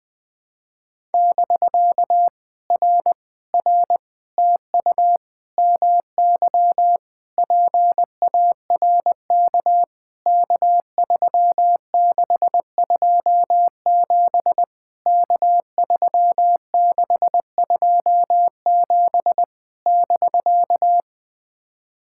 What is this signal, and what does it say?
Morse code, heard as a keyed tone, about 20 wpm, 700 Hz. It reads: <BK> R R TU MY PARK K3627 K3627 <BK>
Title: POTA QSO Practice - 20wpm